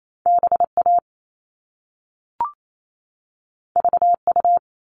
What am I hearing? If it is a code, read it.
Morse code, heard as a keyed tone, about 28 wpm, 700 Hz. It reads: BA VU
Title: All Cut Numbers - 2-Digit 28wpm